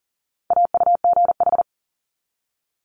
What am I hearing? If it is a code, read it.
Morse code, heard as a keyed tone, about 40 wpm, 700 Hz. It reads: AUGH
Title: Multi-letter Phonemes 40wpm